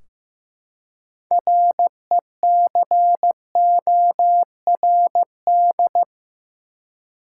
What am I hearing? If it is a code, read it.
Morse code, heard as a keyed tone, about 15 wpm, 700 Hz. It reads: RECORD
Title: Six Letter Words - 15wpm